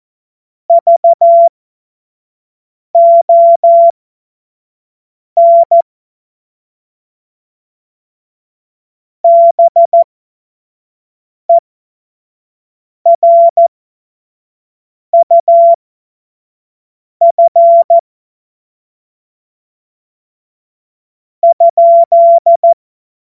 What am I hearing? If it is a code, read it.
Morse code, heard as a keyed tone, about 14 wpm, spaced as for 5 wpm, 680 Hz. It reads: VON BERUF ?
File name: Morsekurs / Witze / Witz 28.November 2023 25 F70